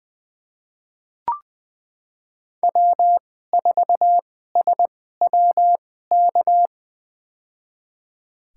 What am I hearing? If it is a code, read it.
Morse code, heard as a keyed tone, about 20 wpm, spaced as for 14 wpm, 700 Hz. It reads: W4SWK
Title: USA Call Signs - 2x Character Spacing - 20wpm